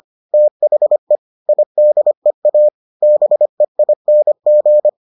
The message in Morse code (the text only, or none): THE IDEA BEING